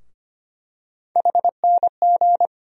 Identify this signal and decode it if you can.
Morse code, heard as a keyed tone, about 25 wpm, 700 Hz. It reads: HNG
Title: USA Call Sign Postfixes - 3 Letters 25wpm